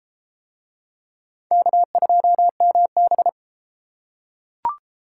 Morse code, heard as a keyed tone, about 32 wpm, 700 Hz. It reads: K2MB